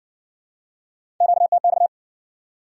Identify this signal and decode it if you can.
Morse code, heard as a keyed tone, about 56 wpm, 700 Hz. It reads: QTX